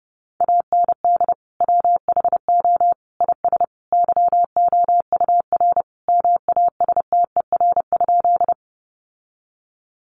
Morse code, heard as a keyed tone, about 30 wpm, 700 Hz. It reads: AND WHO IS YOUR MASTER?
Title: Sentences from Top 1300 Words 30wpm